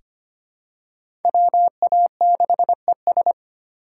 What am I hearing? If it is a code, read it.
Morse code, heard as a keyed tone, about 25 wpm, 700 Hz. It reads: WA6ES
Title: USA Call Signs - 5 Characters - 2x Character Spacing - 25wpm